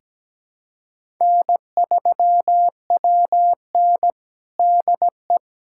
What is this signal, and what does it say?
Morse code, heard as a keyed tone, about 17 wpm, 700 Hz. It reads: N3WN DE
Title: USA Callsign DE Callsign - 4 Characters - 17wpm